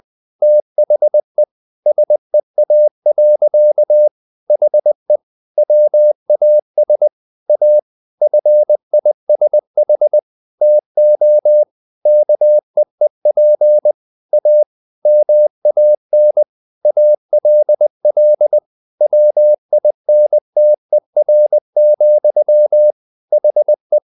Morse code, heard as a keyed tone, about 20 wpm, 600 Hz. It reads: THE SEA. HE WAS A FISH TO KEEP A MAN ALL WINTER, HE
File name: the old man and the sea part 2 20wpm